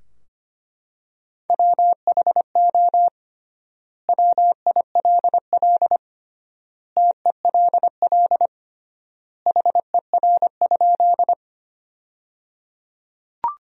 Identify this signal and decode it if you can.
Morse code, heard as a keyed tone, about 25 wpm, 700 Hz. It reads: WHO WILL TELL HER?
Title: 3x Word-Spacing - Sentences from Top 200 Words - 25wpm